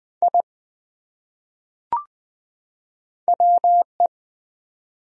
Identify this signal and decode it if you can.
Morse code, heard as a keyed tone, about 20 wpm, 700 Hz. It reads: I WE
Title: Top 100 Words 20wpm